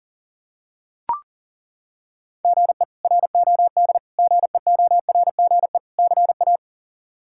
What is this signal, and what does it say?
Morse code, heard as a keyed tone, about 40 wpm, 700 Hz. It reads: GE ROD GEORGE CA
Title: K1USN SST QSO Practice - 40wpm